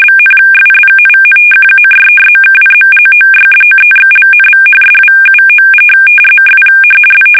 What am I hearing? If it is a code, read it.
Two-tone teleprinter signal, about 850 Hz shift.